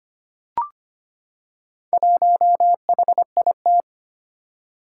Morse code, heard as a keyed tone, about 25 wpm, 700 Hz. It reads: 1HIT
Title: Four Challenging Letter-Number Combinations 25wpm